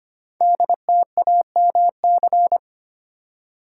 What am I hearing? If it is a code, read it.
Morse code, heard as a keyed tone, about 25 wpm, 700 Hz. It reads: DTAMC
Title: Code Groups - 5 Letters - 25wpm